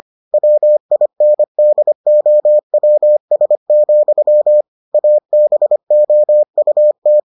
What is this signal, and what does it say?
Morse code, heard as a keyed tone, about 25 wpm, 600 Hz. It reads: WINDOWS, ABOUT